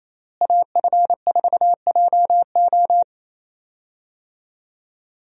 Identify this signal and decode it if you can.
Morse code, heard as a keyed tone, about 28 wpm, 700 Hz. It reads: AF4JO
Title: USA Call Signs - 5 Characters - Repeated 3x - 28wpm